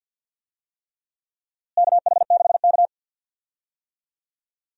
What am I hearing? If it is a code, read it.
Morse code, heard as a keyed tone, about 50 wpm, 700 Hz. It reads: KH6K
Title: DXCC-100 Call Signs - 4 Characters - Repeated 3x - 50wpm